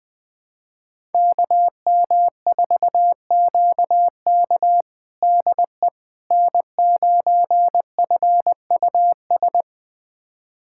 Morse code, heard as a keyed tone, about 20 wpm, 700 Hz. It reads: KM4QK DE N9FUS
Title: USA Callsign DE Callsign - 5 Characters - 20wpm